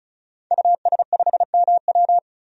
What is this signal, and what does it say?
Morse code, heard as a keyed tone, about 36 wpm, 700 Hz. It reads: US5MW
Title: DXCC-100 Call Signs - 5 Characters - 35wpm